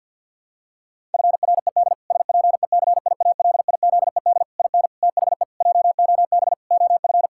Morse code, heard as a keyed tone, about 46 wpm, 700 Hz. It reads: FRED SPECIALIZED IN THE JOB OF
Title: Pangram Challenge v2 - 50wpm